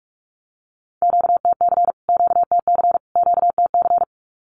Morse code, heard as a keyed tone, about 44 wpm, 700 Hz. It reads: QTC QTC QTC
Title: Q-Codes Repeated 3x 45wpm